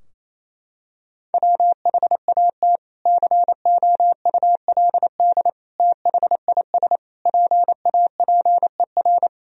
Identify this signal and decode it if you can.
Morse code, heard as a keyed tone, about 28 wpm, 700 Hz. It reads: WHAT COULD THIS PAPER
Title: Sentences from Top 1000 Words - Review Part 2 - 28wpm